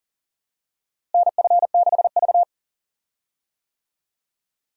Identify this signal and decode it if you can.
Morse code, heard as a keyed tone, about 40 wpm, 700 Hz. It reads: NF6V
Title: USA Call Signs - 4 Characters - Repeated 3x - 40wpm